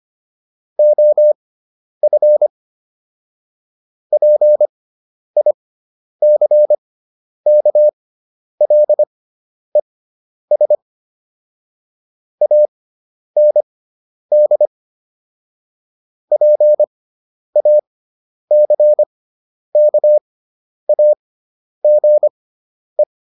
Morse code, heard as a keyed tone, about 25 wpm, spaced as for 10 wpm, 600 Hz. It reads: OF PICKLES AND PACKAGE